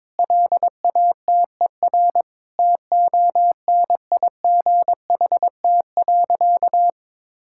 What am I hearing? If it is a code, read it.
Morse code, heard as a keyed tone, about 22 wpm, 700 Hz. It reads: LATER TONIGHT.
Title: Sentences from Top 1000 Words - Review Part 6 - 22wpm